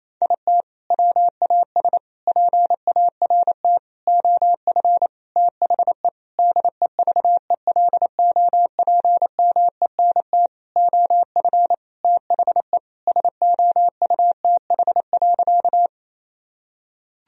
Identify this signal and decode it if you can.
Morse code, heard as a keyed tone, about 28 wpm, 700 Hz. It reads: IT WAS PART OF THE DEVELOPMENT OF THE SOUTH.
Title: Sentences from Top 1000 Words - Review Part 4 - 28wpm